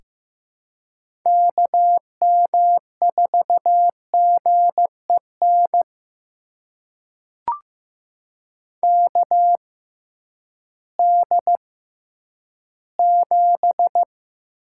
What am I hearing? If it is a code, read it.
Morse code, heard as a keyed tone, about 15 wpm, spaced as for 10 wpm, 700 Hz. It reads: KM4GEN KD7
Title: USA Call Signs - 6 Characters - 6x Character Spacing - 15wpm